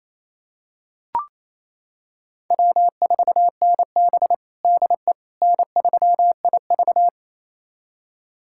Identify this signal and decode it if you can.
Morse code, heard as a keyed tone, about 28 wpm, 700 Hz. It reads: W4NB DE N3IV
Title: USA Callsign DE Callsign - 4 Characters - 28wpm